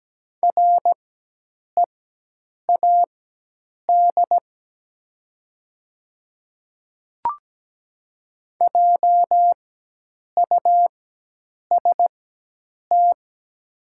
Morse code, heard as a keyed tone, about 17 wpm, spaced as for 8 wpm, 700 Hz. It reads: READ JUST
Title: 4x Character Spacing - Top 200 Words - 17wpm